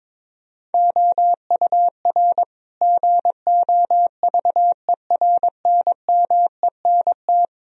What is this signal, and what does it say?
Morse code, heard as a keyed tone, about 22 wpm, 700 Hz. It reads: OUR GOVERNMENT
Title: Sentences from Top 1000 Words - Review Part 3 - 22wpm